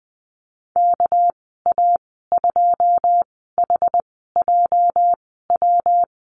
Morse code, heard as a keyed tone, about 20 wpm, spaced as for 14 wpm, 700 Hz. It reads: KA2HJW
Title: 207 Lesson 33g - Letter j - Call Signs 20wpm